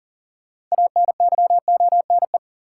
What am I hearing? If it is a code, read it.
Morse code, heard as a keyed tone, about 40 wpm, 700 Hz. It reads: ANYONE
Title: Sentences from Top 900 Words - New Word Drill - 40wpm